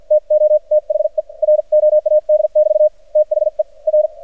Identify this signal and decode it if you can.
Morse code, heard as a keyed tone, about 52 wpm, spaced as for 43 wpm, 610 Hz. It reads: TO THE ROAD<BT> THE R
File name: archived / CRSnet